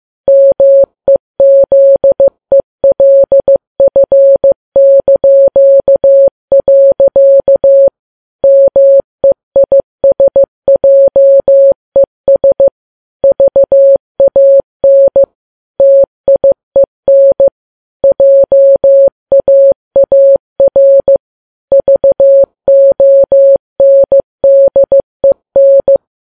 Morse code, heard as a keyed tone, about 15 wpm, 550 Hz. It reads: MEZELF). MEISJES VAN TIEN JAAR VONDEN
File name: Files Web tls / Morse on9fv / morse15kps